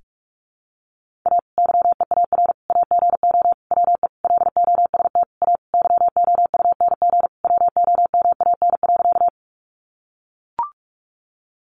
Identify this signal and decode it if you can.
Morse code, heard as a keyed tone, about 44 wpm, 700 Hz. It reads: A YEAR AGO WE LOST A YOUNG WOMAN.
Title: Sentences from Top 1400 Words 45wpm